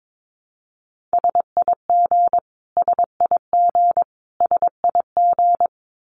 Morse code, heard as a keyed tone, about 22 wpm, 700 Hz. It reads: SIG SIG SIG
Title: Abbreviations Repeated 3x - 22wpm